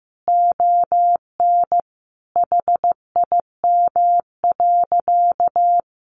Morse code, heard as a keyed tone, about 15 wpm, 700 Hz. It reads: ON HIM.
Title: Sentences from Top 1500 Words 15wpm